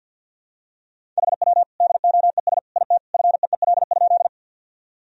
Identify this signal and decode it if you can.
Morse code, heard as a keyed tone, about 46 wpm, 700 Hz. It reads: HW DOES IT FEEL?
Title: Sentences from Top 400 Words and Abbreviations 50wpm